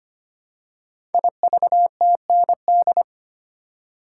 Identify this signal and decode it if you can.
Morse code, heard as a keyed tone, about 25 wpm, 700 Hz. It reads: IVTND
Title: Code Groups - 5 Letters - 25wpm